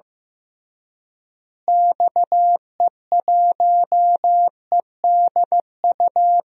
Morse code, heard as a keyed tone, about 16 wpm, 700 Hz. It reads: XE1EDU
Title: DXCC-100 Call Signs - 6 Characters - Repeated 3x - 15wpm